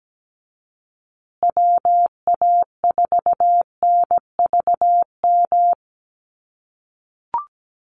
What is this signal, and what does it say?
Morse code, heard as a keyed tone, about 17 wpm, 700 Hz. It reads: WA4NVM
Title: USA Call Signs Repeated 2x 17wpm